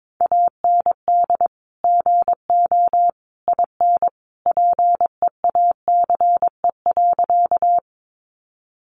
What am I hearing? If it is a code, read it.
Morse code, heard as a keyed tone, about 22 wpm, 700 Hz. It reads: AND GO IN PEACE.
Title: Sentences from Top 1500 Words 22wpm